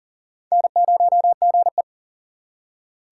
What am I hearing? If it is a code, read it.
Morse code, heard as a keyed tone, about 40 wpm, 700 Hz. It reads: N0GE
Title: USA Call Signs - 4 Characters - 4x Character Spacing - 40wpm